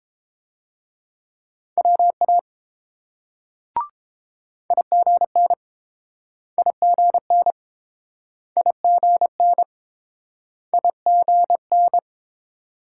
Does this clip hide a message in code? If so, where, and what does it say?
Yes: Morse code from 1.8 to 12.0 s, heard as a keyed tone, about 31 wpm, 700 Hz. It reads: WA IGN IGN IGN IGN